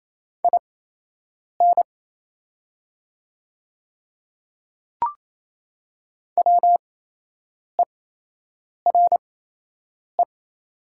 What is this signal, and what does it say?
Morse code, heard as a keyed tone, about 28 wpm, spaced as for 8 wpm, 700 Hz. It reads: IN WERE
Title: 8x Character Spacing - Top 200 Words - 28wpm